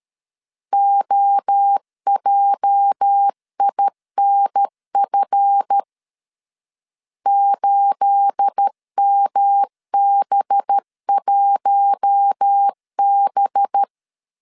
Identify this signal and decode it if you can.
Morse code, heard as a keyed tone, about 13 wpm, 800 Hz. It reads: OJINF 8MB1B
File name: CW / DET / 1 9 1500C S A 12wpm 3800